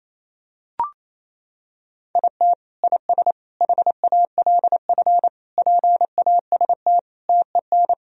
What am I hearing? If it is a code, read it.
Morse code, heard as a keyed tone, about 28 wpm, 700 Hz. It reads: IT IS HALF PAST TEN
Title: Sentences from Top 1000 Words - Review Part 3 - 28wpm